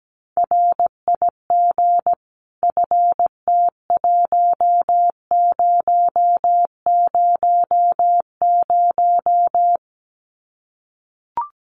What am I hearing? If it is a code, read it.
Morse code, heard as a keyed tone, about 17 wpm, 700 Hz. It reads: RIG FT1000
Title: Common QSO Phrases 17wpm